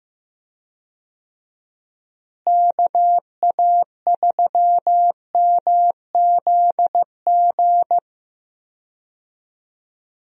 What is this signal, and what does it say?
Morse code, heard as a keyed tone, about 15 wpm, 700 Hz. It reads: KA3MZG
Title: USA Call Signs - 6 Characters - Repeated 3x - 15wpm